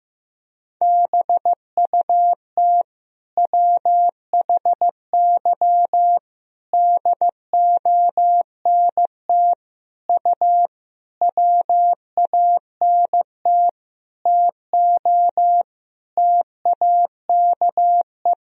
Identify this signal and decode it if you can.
Morse code, heard as a keyed tone, about 16 wpm, 700 Hz. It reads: BUT WHY DONT U WANT TO TAKE
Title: Sentences from Top 300 Words and Abbreviations 15wpm